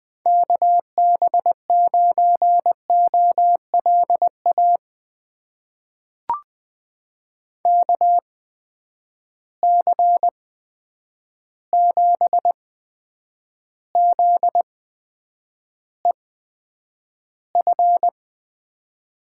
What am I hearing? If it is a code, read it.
Morse code, heard as a keyed tone, about 20 wpm, spaced as for 9 wpm, 700 Hz. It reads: KB9OLA KC7ZEF
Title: USA Call Signs - 6 Characters - 8x Character Spacing - 20wpm